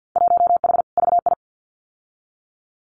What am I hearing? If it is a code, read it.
Morse code, heard as a keyed tone, about 50 wpm, 700 Hz. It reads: 1H VI
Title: Winter Field Day Exchanges Encore 50wpm